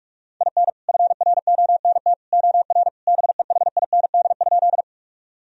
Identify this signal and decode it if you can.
Morse code, heard as a keyed tone, about 46 wpm, 700 Hz. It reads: IN FRONT OR BEHIND?
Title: Sentences from Top 1000 Words - Review Part 2 - 45wpm